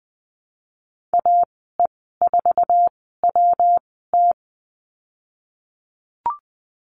Morse code, heard as a keyed tone, about 20 wpm, spaced as for 15 wpm, 700 Hz. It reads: AE4WT
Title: 154 Lesson 25g - Letter p - Call Signs 20wpm